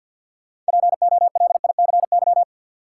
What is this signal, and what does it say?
Morse code, heard as a keyed tone, about 49 wpm, 700 Hz. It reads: POLICY
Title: Sentences from Top 1000 Words - Review Part 5 - New Word Drill - 50wpm